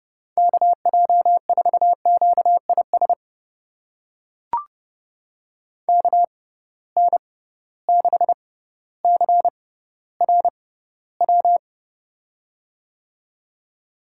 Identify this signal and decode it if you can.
Morse code, heard as a keyed tone, about 30 wpm, spaced as for 15 wpm, 700 Hz. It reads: KJ4QIS KN6CRW
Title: USA Call Signs - 6 Characters - 6x Character Spacing - 30wpm